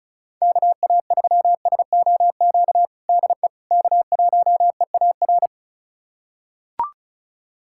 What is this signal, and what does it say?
Morse code, heard as a keyed tone, about 34 wpm, 700 Hz. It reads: KA3SOQ DE K1EAR
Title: USA Callsign DE Callsign 35wpm